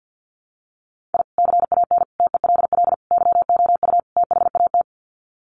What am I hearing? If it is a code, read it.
Morse code, heard as a keyed tone, about 50 wpm, 700 Hz. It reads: I CAN TELL YOU THAT
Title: Sets of 5 Words 50wpm